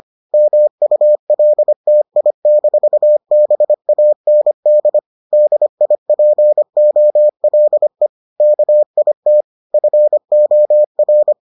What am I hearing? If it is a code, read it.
Morse code, heard as a keyed tone, about 25 wpm, 600 Hz. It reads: MULTI-BAND DIPOLE KIT FOR